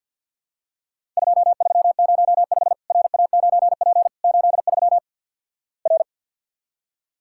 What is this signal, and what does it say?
Morse code, heard as a keyed tone, about 50 wpm, 700 Hz. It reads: 2305 RA9P 83 R
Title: Worked All Europe DX Contest - QTC Exchange with Repeats - 50wpm